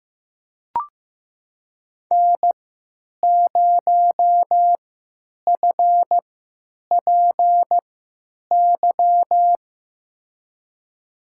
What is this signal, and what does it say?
Morse code, heard as a keyed tone, about 15 wpm, spaced as for 9 wpm, 700 Hz. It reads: N0FPY
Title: USA Call Signs - 5 Characters - 3x Character Spacing - 15wpm